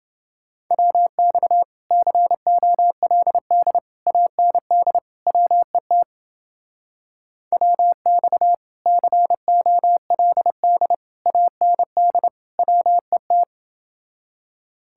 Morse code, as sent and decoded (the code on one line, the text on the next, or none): .-- -..- / -.-. --- .-.. -.. / .- -. -.. / .-- . - / .-- -..- / -.-. --- .-.. -.. / .- -. -.. / .-- . -
WX COLD AND WET WX COLD AND WET